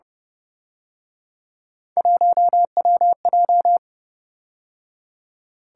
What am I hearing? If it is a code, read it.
Morse code, heard as a keyed tone, about 29 wpm, 700 Hz. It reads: 1WJ